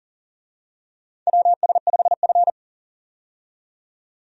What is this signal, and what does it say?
Morse code, heard as a keyed tone, about 40 wpm, 700 Hz. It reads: WS5F
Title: USA Call Signs - 4 Characters - 40wpm